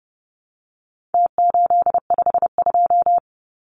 Morse code, heard as a keyed tone, about 30 wpm, 700 Hz. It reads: T852
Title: Numbers - Contest Style - 4-Digit 30wpm